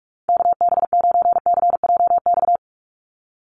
Morse code, heard as a keyed tone, about 46 wpm, 700 Hz. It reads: KB9CJX
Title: USA Call Signs 45wpm